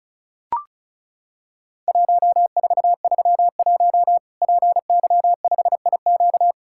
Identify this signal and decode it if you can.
Morse code, heard as a keyed tone, about 34 wpm, 700 Hz. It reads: 1431 PY5IQ
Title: Worked All Europe DX Contest - QTC Exchange - 35wpm